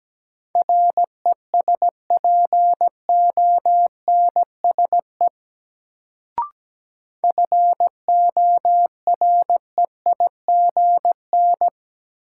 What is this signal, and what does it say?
Morse code, heard as a keyed tone, about 17 wpm, 700 Hz. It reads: RESPONSE FOREIGN
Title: Sentences from Top 1400 Words - New Word Drill - 17wpm